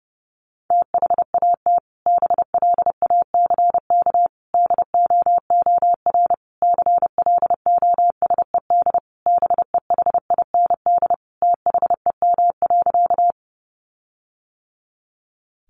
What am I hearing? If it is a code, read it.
Morse code, heard as a keyed tone, about 30 wpm, 700 Hz. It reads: THAT BLACK DOOR CLOSED BEHIND THEM.
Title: Sentences from Top 1400 Words 30wpm